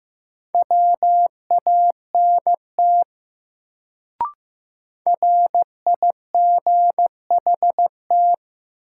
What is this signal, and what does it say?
Morse code, heard as a keyed tone, about 15 wpm, 700 Hz. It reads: WANT RIGHT
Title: Sentences from Top 200 Words - New Word Drill - 15wpm